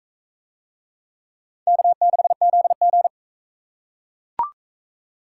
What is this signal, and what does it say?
Morse code, heard as a keyed tone, about 40 wpm, 700 Hz. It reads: K6ZG